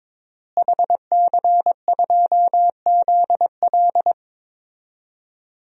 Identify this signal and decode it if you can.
Morse code, heard as a keyed tone, about 22 wpm, 700 Hz. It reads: HC2ZL
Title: DXCC-100 Call Signs - 5 Characters - Repeated 2x - 22wpm